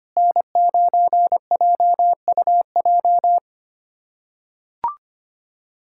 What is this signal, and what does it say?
Morse code, heard as a keyed tone, about 25 wpm, 700 Hz. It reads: N9JUJ
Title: USA Callsign DE Callsign - 5 Characters - 25wpm